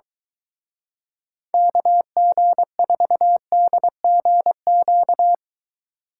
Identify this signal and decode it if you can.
Morse code, heard as a keyed tone, about 23 wpm, 700 Hz. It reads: KG4DGQ